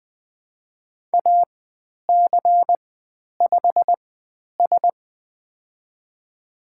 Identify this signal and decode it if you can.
Morse code, heard as a keyed tone, about 20 wpm, spaced as for 10 wpm, 700 Hz. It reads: AC5S
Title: 095 Lesson 16g - Letter c - Call Signs 20wpm